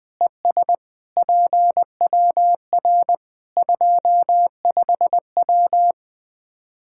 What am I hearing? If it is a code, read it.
Morse code, heard as a keyed tone, about 21 wpm, 700 Hz. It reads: ES PWR 25W
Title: Practice QSO - Exchange 2 - Info - 20wpm